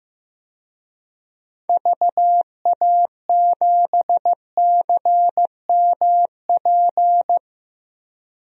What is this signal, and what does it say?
Morse code, heard as a keyed tone, about 15 wpm, 700 Hz. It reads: VA7CMP
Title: DXCC-100 Call Signs - 6 Characters - 15wpm